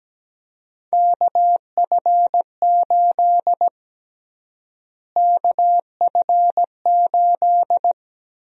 Morse code, heard as a keyed tone, about 17 wpm, 700 Hz. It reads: KF8 KF8
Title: USA Call Sign Prefixes - Repeated 2x - 17wpm